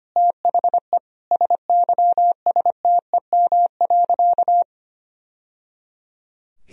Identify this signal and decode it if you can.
Morse code, heard as a keyed tone, about 25 wpm, 700 Hz. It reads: THE SYSTEM.
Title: Sentences from Top 1000 Words - Review Part 3 - 25wpm